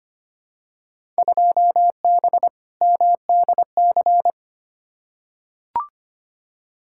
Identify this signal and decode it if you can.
Morse code, heard as a keyed tone, about 25 wpm, 700 Hz. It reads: 2B MDC
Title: Field Day Exchanges - 25wpm